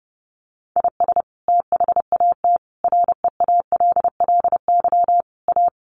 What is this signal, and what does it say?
Morse code, heard as a keyed tone, about 30 wpm, 700 Hz. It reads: IS THAT REALLY A
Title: Sentences from Top 600 Words 30wpm